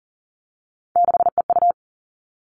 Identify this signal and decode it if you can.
Morse code, heard as a keyed tone, about 40 wpm, 700 Hz. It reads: 6EU
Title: All Cut Numbers - 3-Digit 40wpm